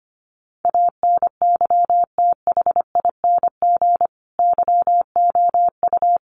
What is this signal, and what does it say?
Morse code, heard as a keyed tone, about 25 wpm, 700 Hz. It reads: ANYTHING YOU
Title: Sentences from Top 1000 Words 25wpm